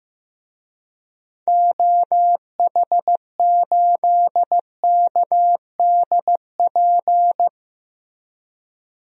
Morse code, heard as a keyed tone, about 16 wpm, 700 Hz. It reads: OH8KDP
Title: DXCC-100 Call Signs - 6 Characters - Repeated 2x - 15wpm